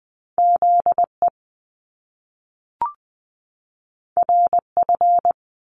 Morse code, heard as a keyed tone, about 20 wpm, 700 Hz. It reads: ZE RF
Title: Top Two Letter Combinations in Words 20wpm